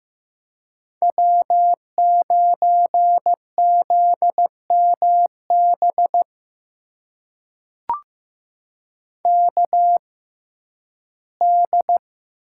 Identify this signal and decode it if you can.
Morse code, heard as a keyed tone, about 15 wpm, spaced as for 11 wpm, 700 Hz. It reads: W9ZMB KD
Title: USA Call Signs - 5 Characters - 6x Character Spacing - 15wpm